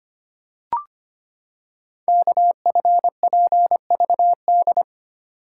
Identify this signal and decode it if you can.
Morse code, heard as a keyed tone, about 25 wpm, 700 Hz. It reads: KFPVD
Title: Code Groups - 5 Letters - 25wpm